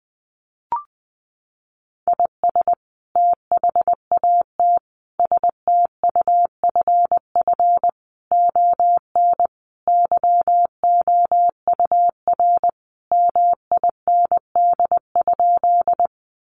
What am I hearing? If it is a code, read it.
Morse code, heard as a keyed tone, about 20 wpm, 700 Hz. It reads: IS THAT STUFF ON YOUR MIND?
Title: Sentences from Top 1400 Words 20wpm